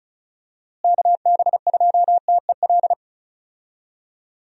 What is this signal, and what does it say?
Morse code, heard as a keyed tone, about 34 wpm, 700 Hz. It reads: KB2TEL
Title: USA Call Signs - 6 Characters - Repeated 3x - 35wpm